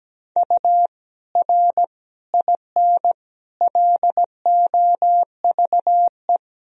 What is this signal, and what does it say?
Morse code, heard as a keyed tone, about 17 wpm, 700 Hz. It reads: U R IN LOVE
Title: Sentences from Top 400 Words and Abbreviations 17wpm